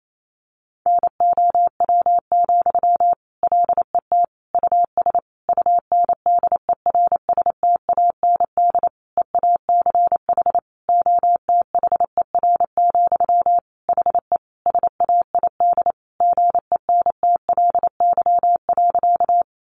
Morse code, heard as a keyed tone, about 28 wpm, 700 Hz. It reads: NOW, LET US UNDERSTAND EACH OTHER, HE SAID GENTLY.
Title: Sentences from Top 1500 Words 28wpm